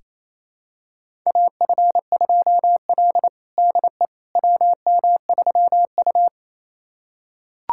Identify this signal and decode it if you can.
Morse code, heard as a keyed tone, about 28 wpm, 700 Hz. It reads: AF2L DE WM3U
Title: USA Callsign DE Callsign - 4 Characters - 28wpm